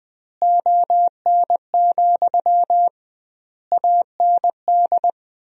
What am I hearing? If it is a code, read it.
Morse code, heard as a keyed tone, about 20 wpm, 700 Hz. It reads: ON, AND
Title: 2x Word-Spacing - Sentences from Top 100 Words - 20wpm